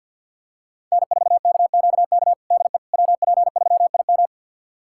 Morse code, heard as a keyed tone, about 50 wpm, 700 Hz. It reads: N4KQK DE WP3IM
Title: USA Callsign DE Callsign - 5 Characters - 50wpm